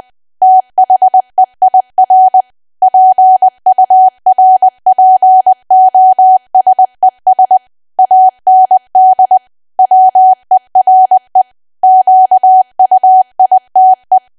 Morse code, heard as a keyed tone, about 20 wpm, 750 Hz. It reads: THEIR PURPOSES AND WERE QUITE
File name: Archive / 20 WPM / 20WPM